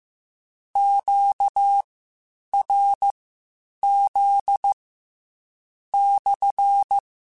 Morse code, heard as a keyed tone, about 15 wpm, spaced as for 8 wpm, 800 Hz. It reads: QRZ /